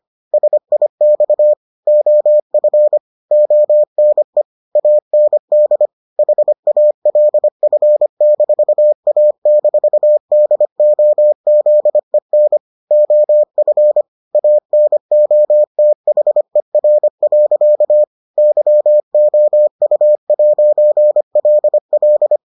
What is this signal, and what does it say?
Morse code, heard as a keyed tone, about 25 wpm, 600 Hz. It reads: SIX OF ONE AND HALF-A-DOZEN OF ANOTHER. YOU'LL